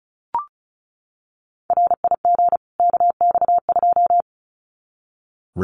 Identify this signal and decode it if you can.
Morse code, heard as a keyed tone, about 34 wpm, 700 Hz. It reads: RIG KX2
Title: Common QSO Phrases 35wpm